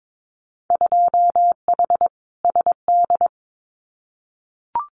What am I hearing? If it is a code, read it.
Morse code, heard as a keyed tone, about 22 wpm, 700 Hz. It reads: 2H SD
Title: Winter Field Day Exchanges 22wpm